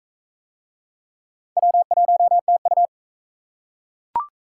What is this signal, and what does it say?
Morse code, heard as a keyed tone, about 40 wpm, 700 Hz. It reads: W1TU